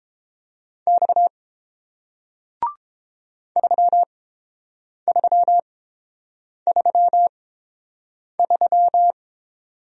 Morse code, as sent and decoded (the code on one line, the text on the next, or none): -..- / ...-- ...-- ...-- ...--
X 3333